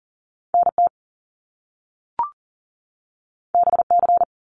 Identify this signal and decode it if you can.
Morse code, heard as a keyed tone, about 40 wpm, 700 Hz. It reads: NT BC
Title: Canadian Provinces and Territories 40wpm